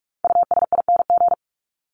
Morse code, heard as a keyed tone, about 44 wpm, 700 Hz. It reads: USING
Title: ING Suffix - Top 100 Words with Base Words 45wpm